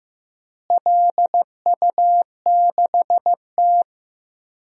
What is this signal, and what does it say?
Morse code, heard as a keyed tone, about 15 wpm, 700 Hz. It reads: LU6T
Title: DXCC-100 Call Signs - 4 Characters - Repeated 2x - 15wpm